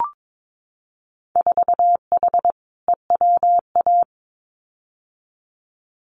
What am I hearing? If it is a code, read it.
Morse code, heard as a keyed tone, about 22 wpm, 700 Hz. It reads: 4H EWA
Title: Winter Field Day Exchanges Encore 22wpm